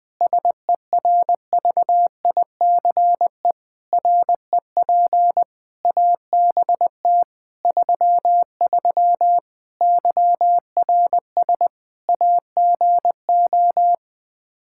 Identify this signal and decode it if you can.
Morse code, heard as a keyed tone, about 21 wpm, 700 Hz. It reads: SERVICE REP ABT 33 YRS AGO
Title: Practice QSO - Exchange 3 - More Info - 20wpm